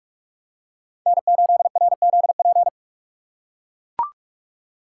Morse code, heard as a keyed tone, about 40 wpm, 700 Hz. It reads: N8RZP